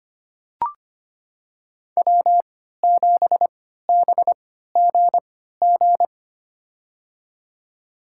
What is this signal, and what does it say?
Morse code, heard as a keyed tone, about 25 wpm, spaced as for 14 wpm, 700 Hz. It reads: W7BGG
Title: USA Call Signs - 3x Character Spacing - 25wpm